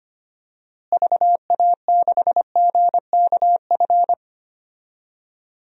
Morse code, heard as a keyed tone, about 25 wpm, 700 Hz. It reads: VA6GKF
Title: DXCC-100 Call Signs - 6 Characters - Repeated 3x - 25wpm